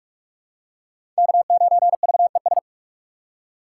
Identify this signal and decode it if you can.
Morse code, heard as a keyed tone, about 42 wpm, 700 Hz. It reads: K9VES
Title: USA Call Signs - 5 Characters - 6x Character Spacing - 45wpm